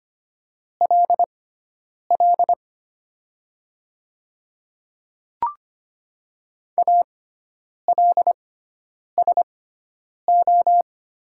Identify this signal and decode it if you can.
Morse code, heard as a keyed tone, about 25 wpm, spaced as for 9 wpm, 700 Hz. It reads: LL ALSO
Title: 6x Character Spacing - Top 500 Words - 25wpm